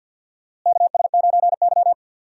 Morse code, heard as a keyed tone, about 47 wpm, 700 Hz. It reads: KS9Y
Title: USA Call Signs - 4 Characters - Repeated 3x - 50wpm